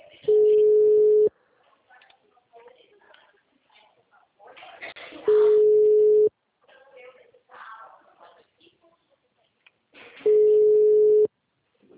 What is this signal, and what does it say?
Ringback tone (Europe, 425 Hz), with speech.